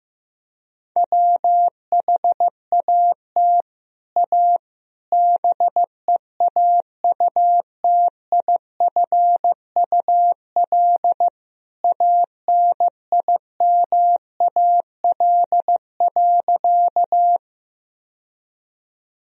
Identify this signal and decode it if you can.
Morse code, heard as a keyed tone, about 15 wpm, 700 Hz. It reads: WHAT A BEAUTIFUL ANIMAL.
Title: Sentences from Top 1000 Words - Review Part 1 - 15wpm